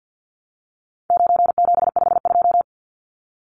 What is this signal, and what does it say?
Morse code, heard as a keyed tone, about 49 wpm, 700 Hz. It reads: 9752
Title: Numbers - 4-Digit 50wpm